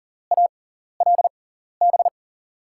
Morse code, heard as a keyed tone, about 37 wpm, spaced as for 13 wpm, 700 Hz. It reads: ALB
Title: USA Call Signs - 6 Characters - 6x Character Spacing - 40wpm